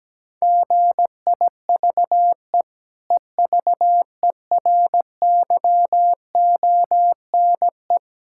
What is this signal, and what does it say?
Morse code, heard as a keyed tone, about 17 wpm, 700 Hz. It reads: GIVE EVERYONE
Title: Sentences from Top 1000 Words - Review Part 6 - 17wpm